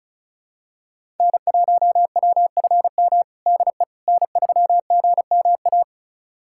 Morse code, heard as a keyed tone, about 36 wpm, 700 Hz. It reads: N1WFM DE N3GMA
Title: USA Callsign DE Callsign - 5 Characters - 35wpm